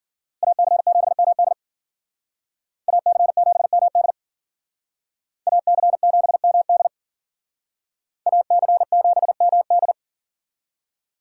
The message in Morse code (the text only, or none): AC7MD AC7MD AC7MD AC7MD